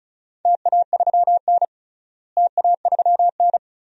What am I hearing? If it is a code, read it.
Morse code, heard as a keyed tone, about 36 wpm, 700 Hz. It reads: TA3N TA3N
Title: DXCC-100 Call Signs - 4 Characters - Repeated 2x - 35wpm